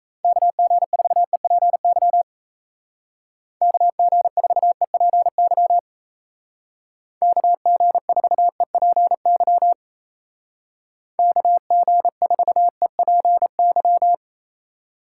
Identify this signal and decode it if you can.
Morse code, heard as a keyed tone, about 35 wpm, 700 Hz. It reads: KG4EPY KG4EPY KG4EPY KG4EPY
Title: Speed Racing - USA 2x3 Call Signs 28wpm